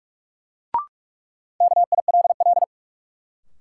Morse code, heard as a keyed tone, about 44 wpm, 700 Hz. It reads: KILL
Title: Sentences from Top 1400 Words - New Word Drill - 45wpm